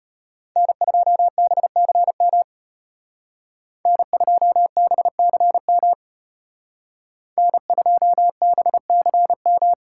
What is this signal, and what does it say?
Morse code, heard as a keyed tone, about 30 wpm, 700 Hz. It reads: N2BCM N2BCM N2BCM
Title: Speed Racing - USA 1x3 Call Signs 25wpm